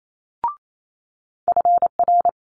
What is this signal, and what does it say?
Morse code, heard as a keyed tone, about 28 wpm, 700 Hz. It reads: FR
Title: Multi-letter Phonemes Quick Fire 28wpm